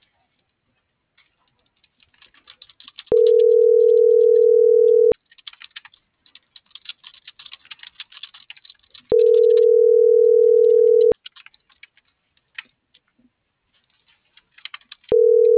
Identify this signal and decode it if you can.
Ringback tone (North America), 440 + 480 Hz.